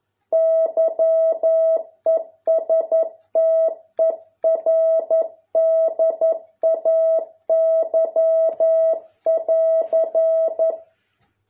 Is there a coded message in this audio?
Yes: Morse code from 0.3 to 10.7 s, heard as a keyed tone, about 11 wpm, 650 Hz. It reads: YESTERDAY<AR>